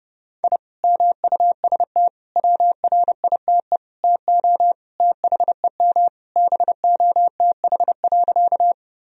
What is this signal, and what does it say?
Morse code, heard as a keyed tone, about 30 wpm, 700 Hz. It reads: I MUST WRITE TO THEM BOTH.
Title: Sentences from Top 1000 Words - Review Part 1 - 30wpm